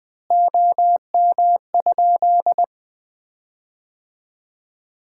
Morse code, heard as a keyed tone, about 20 wpm, 700 Hz. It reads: OM?
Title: Common QSO Phrases - 2x Word-Spacing - 20wpm